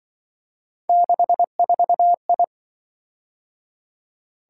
Morse code, heard as a keyed tone, about 24 wpm, 700 Hz. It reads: 64I
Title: Speed-Racing - Three Letter-Number Combinations - 20wpm